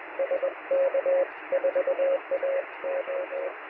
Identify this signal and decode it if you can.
Morse code, heard as a keyed tone, about 21 wpm, 540 Hz, with radio noise. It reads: SK4AO
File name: sm7lcb / 1022z sk4ao 23cm